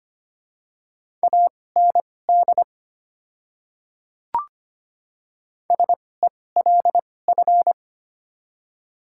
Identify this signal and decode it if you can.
Morse code, heard as a keyed tone, about 25 wpm, spaced as for 18 wpm, 690 Hz. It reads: AND SELF
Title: 2x Character Spacing - Top 300 Words - 25wpm